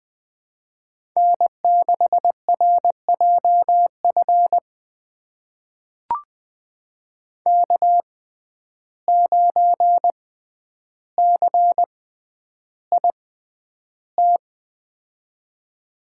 Morse code, heard as a keyed tone, about 20 wpm, spaced as for 10 wpm, 700 Hz. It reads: N6RJF K9CIT
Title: USA Call Signs - 5 Characters - 6x Character Spacing - 20wpm